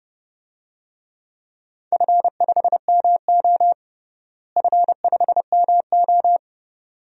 Morse code, heard as a keyed tone, about 30 wpm, 700 Hz. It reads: F5MO F5MO
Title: DXCC-100 Call Signs - 4 Characters - Repeated 2x - 30wpm